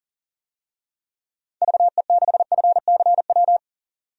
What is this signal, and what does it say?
Morse code, heard as a keyed tone, about 40 wpm, 700 Hz. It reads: VE6FCW
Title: DXCC-100 Call Signs - 6 Characters - 40wpm